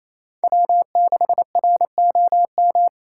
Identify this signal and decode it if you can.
Morse code, heard as a keyed tone, about 34 wpm, 700 Hz. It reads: W6ROM
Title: Speed Racing - USA 1x3 Call Signs 28wpm